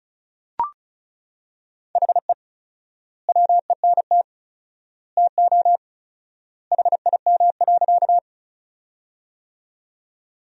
Morse code, heard as a keyed tone, about 34 wpm, 700 Hz. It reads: HE WENT TO HIM.
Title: 4x Word-Spacing - Sentences from Top 200 Words - 35wpm